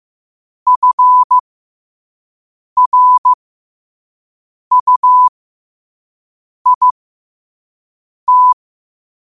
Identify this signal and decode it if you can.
Morse code, heard as a keyed tone, about 15 wpm, spaced as for 5 wpm, 1000 Hz. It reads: FRUIT
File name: qotdfiles / 5wpm